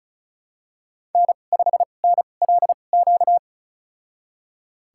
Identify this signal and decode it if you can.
Morse code, heard as a keyed tone, about 36 wpm, spaced as for 25 wpm, 700 Hz. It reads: N5NLQ
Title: USA Call Signs - 5 Characters - 2x Character Spacing - 35wpm